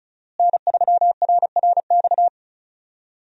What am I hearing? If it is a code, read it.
Morse code, heard as a keyed tone, about 36 wpm, 700 Hz. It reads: N3RRX
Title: USA Call Signs - 5 Characters - Repeated 2x - 35wpm